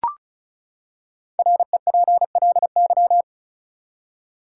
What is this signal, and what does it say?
Morse code, heard as a keyed tone, about 34 wpm, 700 Hz. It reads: REPLY
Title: Five Letter Words - 35wpm